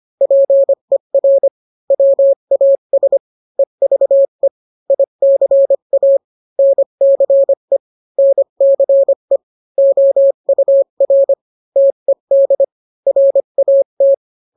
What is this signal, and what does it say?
Morse code, heard as a keyed tone, about 25 wpm, 550 Hz. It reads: PER WAS EVE ICA NCE NCE OUR TED RAT